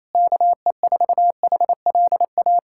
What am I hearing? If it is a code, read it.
Morse code, heard as a keyed tone, about 28 wpm, 700 Hz. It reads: KE4HLA
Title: USA Callsign DE Callsign - 6 Characters - 28wpm